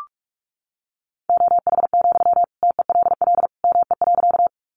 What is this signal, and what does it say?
Morse code, heard as a keyed tone, about 44 wpm, 700 Hz. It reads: OH, TELL ME.
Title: Sentences from Top 500 Words 45wpm